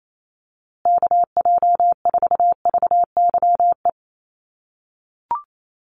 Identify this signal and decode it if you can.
Morse code, heard as a keyed tone, about 28 wpm, 700 Hz. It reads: KJ4VYE
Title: USA Call Signs Repeated 2x 28wpm